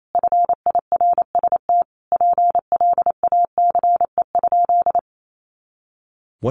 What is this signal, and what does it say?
Morse code, heard as a keyed tone, about 28 wpm, 700 Hz. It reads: FIRST PLACE?
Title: Sentences from Top 200 Words 28wpm